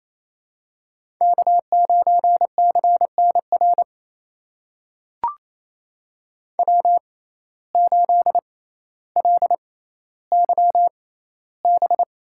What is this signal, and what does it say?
Morse code, heard as a keyed tone, about 27 wpm, spaced as for 14 wpm, 700 Hz. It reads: K9CNR W8LYB
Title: USA Call Signs - 5 Characters - 6x Character Spacing - 28wpm